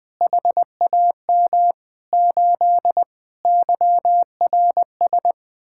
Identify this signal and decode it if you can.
Morse code, heard as a keyed tone, about 20 wpm, 700 Hz. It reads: HAM 8 YRS
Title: Speed-Racing Common QSO Phrases - 17wpm